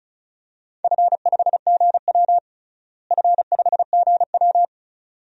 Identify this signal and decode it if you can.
Morse code, heard as a keyed tone, about 36 wpm, 690 Hz. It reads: F5GW F5GW
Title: DXCC-100 Call Signs - 4 Characters - Repeated 2x - 35wpm